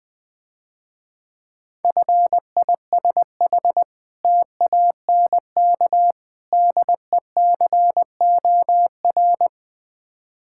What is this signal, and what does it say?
Morse code, heard as a keyed tone, about 20 wpm, 690 Hz. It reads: FISH TANK DECOR
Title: Difficult Words - Speed-Racing with Hints - 20wpm